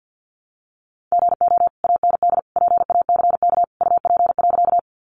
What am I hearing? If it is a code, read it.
Morse code, heard as a keyed tone, about 50 wpm, 700 Hz. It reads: GO AND PACK UP.
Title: Sentences from Top 1300 Words 50wpm